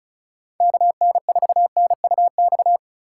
Morse code, heard as a keyed tone, about 33 wpm, 700 Hz. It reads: KN4NUX
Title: Speed Racing - USA 2x3 Call Signs 35wpm